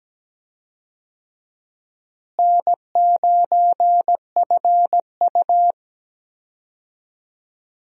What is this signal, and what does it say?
Morse code, heard as a keyed tone, about 17 wpm, 700 Hz. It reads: N9FU
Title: USA Call Signs - 4 Characters - Repeated 3x - 17wpm